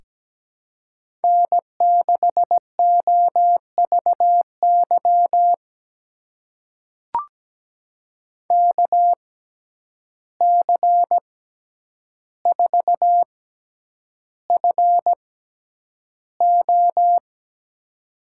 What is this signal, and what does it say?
Morse code, heard as a keyed tone, about 17 wpm, spaced as for 9 wpm, 700 Hz. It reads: N6OVY KC4FO
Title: USA Call Signs - 6x Character Spacing - 17wpm